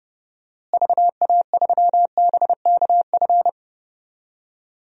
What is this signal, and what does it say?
Morse code, heard as a keyed tone, about 32 wpm, 700 Hz. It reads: VA3BKF
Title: DXCC-100 Call Signs - 6 Characters - Repeated 3x - 30wpm